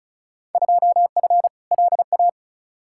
Morse code, heard as a keyed tone, about 36 wpm, 700 Hz. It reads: 2F LA
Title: Field Day Exchanges - 35wpm